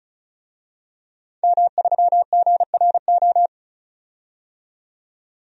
Morse code, heard as a keyed tone, about 34 wpm, 700 Hz. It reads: M3GRO
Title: DXCC-100 Call Signs - 5 Characters - 35wpm